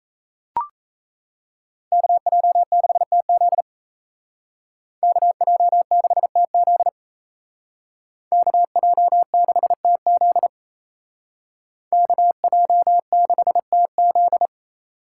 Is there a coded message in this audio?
Yes: Morse code from 1.9 to 14.5 s, heard as a keyed tone, about 36 wpm, 700 Hz. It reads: KJ6TZ KJ6TZ KJ6TZ KJ6TZ